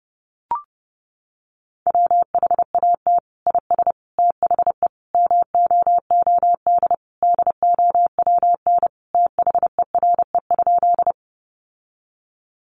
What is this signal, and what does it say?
Morse code, heard as a keyed tone, about 28 wpm, 700 Hz. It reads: WHAT IS THE MOOD DOWN THERE?
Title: Sentences from Top 1100 Words 30wpm